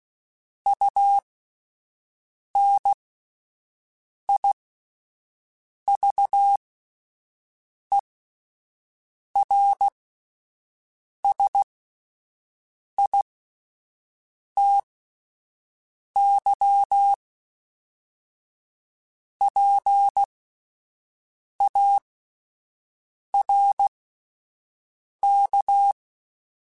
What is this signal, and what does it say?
Morse code, heard as a keyed tone, about 16 wpm, spaced as for 5 wpm, 800 Hz. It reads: UNIVERSITY PARK